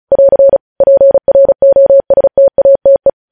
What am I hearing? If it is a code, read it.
Morse code, heard as a keyed tone, about 36 wpm, 550 Hz. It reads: <AR> PROSTATE
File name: podcast / News headlines-1x-35x35